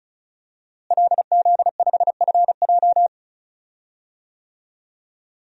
Morse code, heard as a keyed tone, about 36 wpm, 700 Hz. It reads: LZ5FJ
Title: DXCC-100 Call Signs - 5 Characters - Repeated 2x - 35wpm